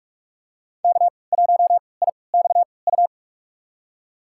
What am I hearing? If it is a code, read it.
Morse code, heard as a keyed tone, about 41 wpm, spaced as for 25 wpm, 700 Hz. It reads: K1IXU
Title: USA Call Signs - 5 Characters - 3x Character Spacing - 45wpm